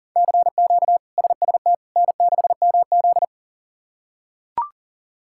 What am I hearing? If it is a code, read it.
Morse code, heard as a keyed tone, about 40 wpm, 700 Hz. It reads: CQ SST N6MZ
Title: K1USN SST QSO Practice - 40wpm